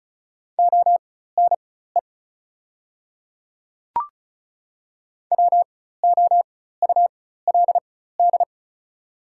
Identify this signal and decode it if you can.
Morse code, heard as a keyed tone, about 36 wpm, spaced as for 16 wpm, 700 Hz. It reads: ONE WOULD
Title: 4x Character Spacing - Top 100 Words - 35wpm